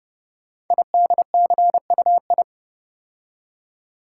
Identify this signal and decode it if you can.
Morse code, heard as a keyed tone, about 30 wpm, 700 Hz. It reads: IDCUI
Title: Code Groups - 5 Letter:Numbers - 30wpm